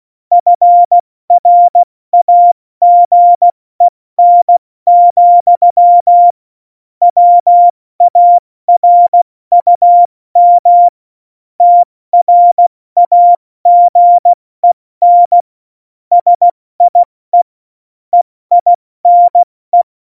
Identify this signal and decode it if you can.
Morse code, heard as a keyed tone, about 16 wpm, 700 Hz. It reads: FRAGEN, WARUM TRAGEN SIE EINE
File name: Morsekurs / Witze / Witz 16. April 2019, 70 F80